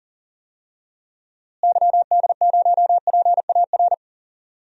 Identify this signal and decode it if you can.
Morse code, heard as a keyed tone, about 40 wpm, 700 Hz. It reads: YD0PAR
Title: DXCC-100 Call Signs - 6 Characters - Repeated 2x - 40wpm